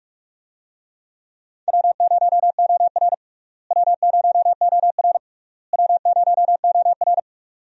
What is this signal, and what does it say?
Morse code, heard as a keyed tone, about 41 wpm, 700 Hz. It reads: W0OR W0OR W0OR
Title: USA Call Signs - 4 Characters - Repeated 3x - 45wpm